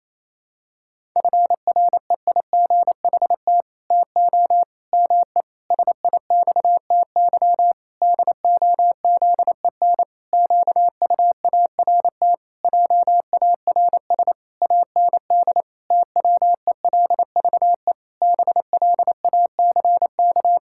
Morse code, heard as a keyed tone, about 28 wpm, 700 Hz. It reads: FREIGHT TO ME SIXTY DOZEN QUART JARS AND TWELVE BLACK
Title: Pangram Challenge v2 - 28wpm